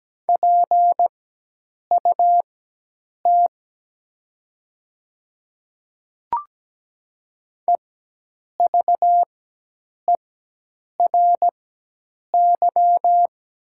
Morse code, heard as a keyed tone, about 17 wpm, spaced as for 8 wpm, 700 Hz. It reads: PUT EVERY
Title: 4x Character Spacing - Top 200 Words - 17wpm